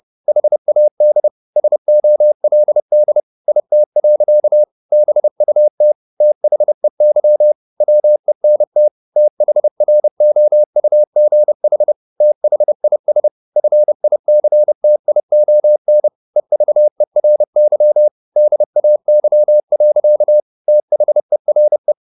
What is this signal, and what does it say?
Morse code, heard as a keyed tone, about 30 wpm, 600 Hz. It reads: HAD SOLD IT. BUT THEY WENT THROUGH THIS FICTION EVERY DAY. THERE
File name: the old man and the sea / the old man and the sea 1 30wpm